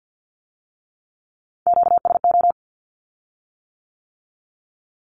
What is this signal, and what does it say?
Morse code, heard as a keyed tone, about 50 wpm, 700 Hz. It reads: QSO
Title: Q-Codes with Spoken Meaning - 50wpm